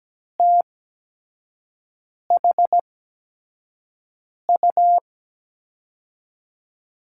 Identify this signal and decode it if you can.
Morse code, heard as a keyed tone, about 17 wpm, spaced as for 5 wpm, 700 Hz. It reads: THU
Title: USA Call Signs - 5 Characters - 8x Character Spacing - 17wpm